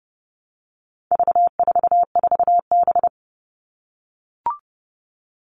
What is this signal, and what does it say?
Morse code, heard as a keyed tone, about 30 wpm, 700 Hz. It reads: V44B
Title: All Cut Numbers - 4-Digit 30wpm